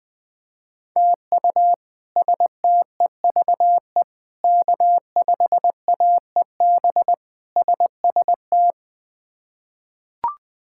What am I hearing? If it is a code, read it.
Morse code, heard as a keyed tone, about 20 wpm, 700 Hz. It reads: TU STEVE K5AEB SST
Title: K1USN SST QSO Practice - 20wpm